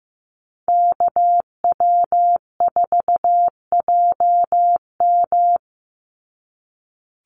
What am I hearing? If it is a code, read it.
Morse code, heard as a keyed tone, about 15 wpm, 700 Hz. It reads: KW4JM
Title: NA SOTA Activator Callsigns 15wpm